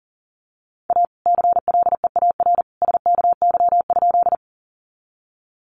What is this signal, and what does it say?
Morse code, heard as a keyed tone, about 40 wpm, 700 Hz. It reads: A CLEAR SKY?
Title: Sentences from Top 1200 Words 40wpm